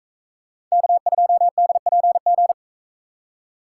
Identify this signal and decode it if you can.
Morse code, heard as a keyed tone, about 44 wpm, 700 Hz. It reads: K2DPG